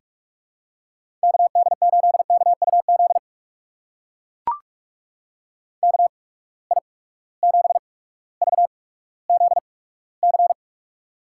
Morse code, heard as a keyed tone, about 44 wpm, spaced as for 19 wpm, 700 Hz. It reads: KD8KUZ KI7VZC
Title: USA Call Signs - 6 Characters - 8x Character Spacing - 45wpm